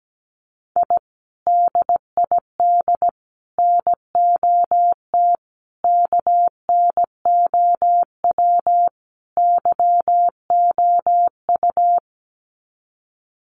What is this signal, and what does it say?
Morse code, heard as a keyed tone, about 17 wpm, 700 Hz. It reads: I DID NOT KNOW YOU
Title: Sets of 5 Words Encore2 17wpm